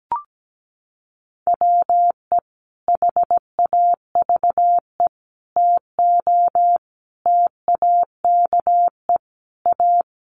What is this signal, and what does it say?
Morse code, heard as a keyed tone, about 17 wpm, 700 Hz. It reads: WE HAVE TO TAKE A
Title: Sets of 5 Words Encore 17wpm